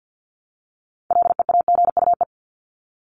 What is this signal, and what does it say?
Morse code, heard as a keyed tone, about 50 wpm, 700 Hz. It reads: LEAGUE
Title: Top 1751-2000 Words 50wpm